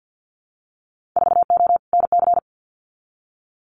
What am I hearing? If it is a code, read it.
Morse code, heard as a keyed tone, about 50 wpm, 700 Hz. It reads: 4O NC
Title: Winter Field Day Exchanges 50wpm